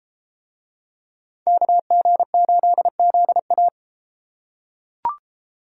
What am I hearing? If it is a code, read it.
Morse code, heard as a keyed tone, about 31 wpm, 700 Hz. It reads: KG8ZA